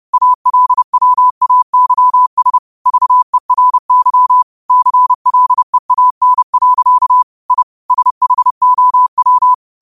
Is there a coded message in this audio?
Yes: Morse code from 0.1 to 9.5 s, heard as a keyed tone, about 30 wpm, 1000 Hz. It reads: ALWAYS VERY CLEAN. I SHOW